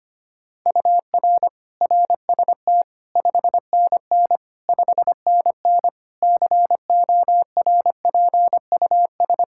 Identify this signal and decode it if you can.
Morse code, heard as a keyed tone, about 25 wpm, 700 Hz. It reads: UR RST 5NN 5NN CORPUS
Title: K1USN SST QSO Practice 25wpm